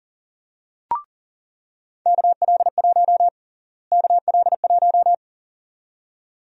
Morse code, heard as a keyed tone, about 38 wpm, 700 Hz. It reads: KL1 KL1
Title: USA Call Sign Prefixes - Repeated 2x - 40wpm